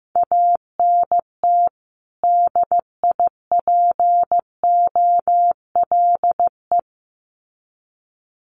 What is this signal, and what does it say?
Morse code, heard as a keyed tone, about 15 wpm, 700 Hz. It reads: ANT DIPOLE
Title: Common QSO Phrases 15wpm